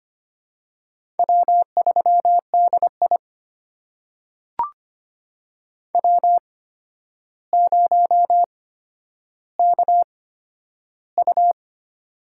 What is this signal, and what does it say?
Morse code, heard as a keyed tone, about 25 wpm, spaced as for 11 wpm, 700 Hz. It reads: W3DI W0KU
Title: USA Call Signs - 4 Characters - 8x Character Spacing - 25wpm